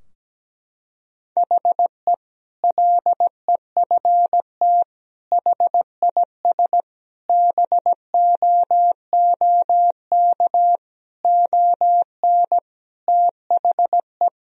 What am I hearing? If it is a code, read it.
Morse code, heard as a keyed tone, about 17 wpm, 700 Hz. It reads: HE LEFT HIS BOOK ON THE
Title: Sentences from Top 1000 Words - Review Part 2 - 17wpm